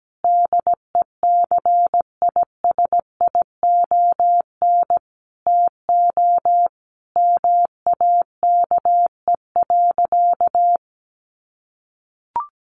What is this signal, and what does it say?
Morse code, heard as a keyed tone, about 17 wpm, 700 Hz. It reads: DECISION TO MAKE.
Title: Sentences from Top 1100 Words 17wpm